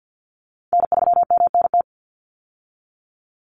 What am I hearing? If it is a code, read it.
Morse code, heard as a keyed tone, about 50 wpm, 700 Hz. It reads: N3MNT
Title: Call Signs Repeated 3x - 50wpm